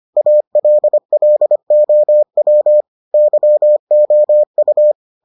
Morse code, heard as a keyed tone, about 25 wpm, 600 Hz. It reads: ALLOW YOU